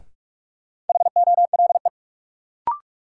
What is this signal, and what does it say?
Morse code, heard as a keyed tone, about 41 wpm, 700 Hz. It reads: HOLE
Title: Sentences from Top 1600 Words - New Word Drill - 45wpm